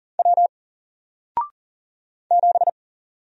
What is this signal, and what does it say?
Morse code, heard as a keyed tone, about 40 wpm, 700 Hz. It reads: W 7
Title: Single Character (Letter:Number:Punctuation:Prosign) Alt 40wpm